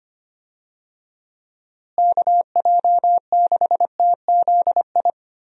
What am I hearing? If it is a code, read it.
Morse code, heard as a keyed tone, about 25 wpm, 700 Hz. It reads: KJ6TZI
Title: USA Call Signs - 6 Characters - Repeated 2x - 25wpm